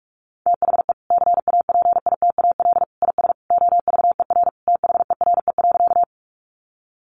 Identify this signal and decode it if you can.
Morse code, heard as a keyed tone, about 44 wpm, 700 Hz. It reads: THE CAPITAL IS OVER THERE.
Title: Sentences from Top 1000 Words 45wpm